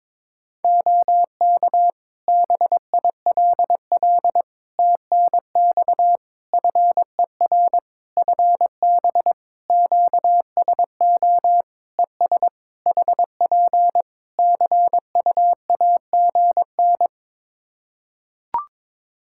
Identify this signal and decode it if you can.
Morse code, heard as a keyed tone, about 22 wpm, 700 Hz. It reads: OK BILL TNX FER FB QSO ES HP CUAGN
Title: Practice QSO - Exchange 4 - Ending - 22wpm